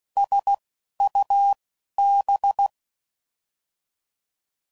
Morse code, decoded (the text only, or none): S U B